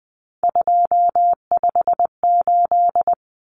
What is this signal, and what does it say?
Morse code, heard as a keyed tone, about 20 wpm, 700 Hz. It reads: 258
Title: Numbers - 3-Digit 20wpm